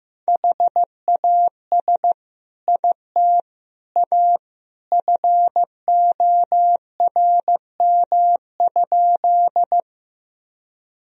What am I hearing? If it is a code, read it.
Morse code, heard as a keyed tone, about 16 wpm, 700 Hz. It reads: HAS IT A FORM?
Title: Sentences from Top 1000 Words - Review Part 1 - 15wpm